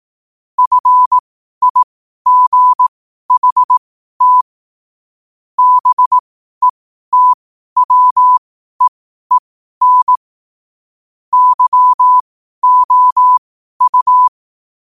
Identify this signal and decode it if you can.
Morse code, heard as a keyed tone, about 18 wpm, spaced as for 12 wpm, 1000 Hz. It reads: FIGHT BETWEEN YOU
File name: qotdfiles / 12wpm